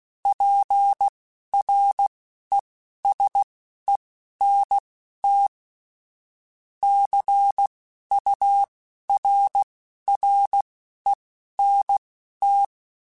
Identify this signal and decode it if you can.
Morse code, heard as a keyed tone, about 16 wpm, spaced as for 11 wpm, 800 Hz. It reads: PRESENT CURRENT
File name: Technical Subjects / Technical Subjects-mp3s / XL @10